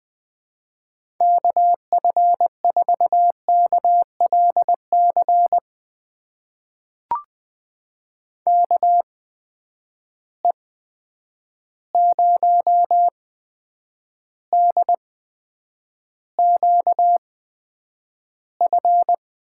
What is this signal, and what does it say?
Morse code, heard as a keyed tone, about 20 wpm, spaced as for 9 wpm, 700 Hz. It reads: KF4KLC KE0DQF
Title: USA Call Signs - 6 Characters - 8x Character Spacing - 20wpm